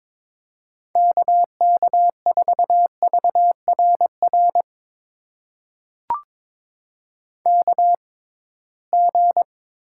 Morse code, heard as a keyed tone, about 22 wpm, spaced as for 17 wpm, 700 Hz. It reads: KK4VRR KG
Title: USA Call Signs - 6 Characters - 6x Character Spacing - 22wpm